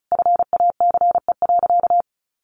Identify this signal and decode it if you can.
Morse code, heard as a keyed tone, about 34 wpm, 700 Hz. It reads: FACE.
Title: Sentences from Top 1100 Words 35wpm